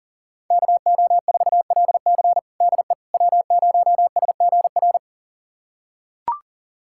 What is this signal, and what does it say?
Morse code, heard as a keyed tone, about 40 wpm, 700 Hz. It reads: KO4LC DE W0SGR
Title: USA Callsign DE Callsign - 5 Characters - 40wpm